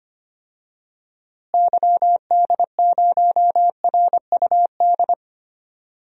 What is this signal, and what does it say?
Morse code, heard as a keyed tone, about 25 wpm, 700 Hz. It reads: YD0RUD
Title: DXCC-100 Call Signs - 6 Characters - Repeated 2x - 25wpm